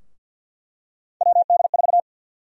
Morse code, heard as a keyed tone, about 49 wpm, 700 Hz. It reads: WD4
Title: USA Call Sign Prefixes - Repeated 2x - 50wpm